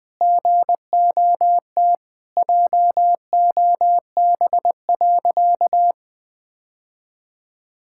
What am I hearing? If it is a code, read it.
Morse code, heard as a keyed tone, about 20 wpm, 700 Hz. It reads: GOT JOB.
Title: Sentences with 4-Word Phrases - 20wpm